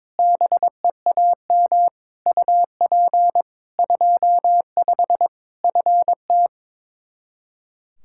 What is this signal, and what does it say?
Morse code, heard as a keyed tone, about 22 wpm, 700 Hz. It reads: BEAM UP 25 FT
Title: Practice QSO - Exchange 2 - Info - 22wpm